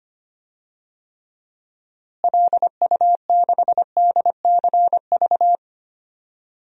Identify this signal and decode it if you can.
Morse code, heard as a keyed tone, about 25 wpm, 700 Hz. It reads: LU6DCV
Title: DXCC-100 Call Signs - 6 Characters - Repeated 2x - 25wpm